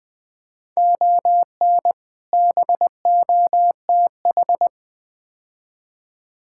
Morse code, heard as a keyed tone, about 20 wpm, 700 Hz. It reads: ON BOTH
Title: Build A Phrase - 4-Word Phrases - 20wpm